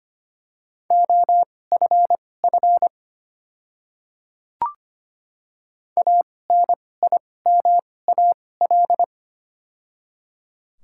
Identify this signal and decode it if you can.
Morse code, heard as a keyed tone, about 25 wpm, spaced as for 18 wpm, 700 Hz. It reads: OFF ANIMAL
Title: 2x Character Spacing - Top 200 Words - 25wpm